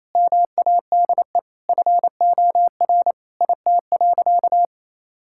Morse code, heard as a keyed tone, about 28 wpm, 700 Hz. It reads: MADE FOR IT.
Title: Sentences from Top 1600 Words 28wpm